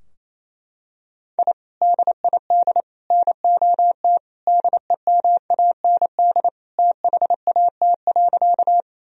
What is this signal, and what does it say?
Morse code, heard as a keyed tone, about 27 wpm, 700 Hz. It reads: I DID NOT DEMAND THAT.
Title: Sentences from Top 1000 Words - Review Part 4 - 28wpm